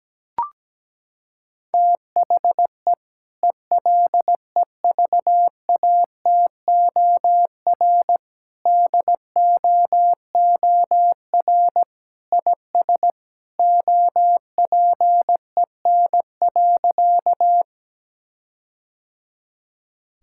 Morse code, heard as a keyed tone, about 17 wpm, 700 Hz. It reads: THE ELEVATOR DOOR IS OPEN.
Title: Sentences from Top 1600 Words 17wpm